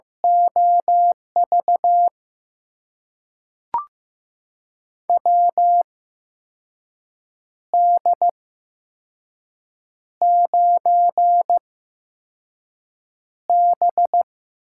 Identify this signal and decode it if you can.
Morse code, heard as a keyed tone, about 15 wpm, spaced as for 5 wpm, 700 Hz. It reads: OV WD9B
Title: USA Call Signs - 6 Characters - 8x Character Spacing - 15wpm